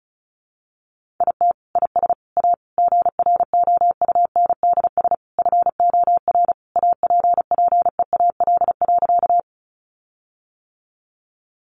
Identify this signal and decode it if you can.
Morse code, heard as a keyed tone, about 34 wpm, 700 Hz. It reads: IT IS A GROUNDS FOR APPEAL.
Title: Sentences from Top 1300 Words 35wpm